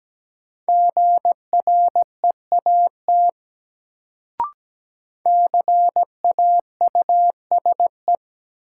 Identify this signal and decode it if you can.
Morse code, heard as a keyed tone, about 17 wpm, 700 Hz. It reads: GREAT CAUSE
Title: Sentences from Top 200 Words - New Word Drill - 17wpm